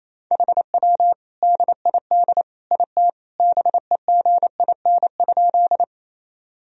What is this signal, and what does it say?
Morse code, heard as a keyed tone, about 28 wpm, 700 Hz. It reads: HW DID IT BEGIN?
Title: Sentences from Top 300 Words and Abbreviations 28wpm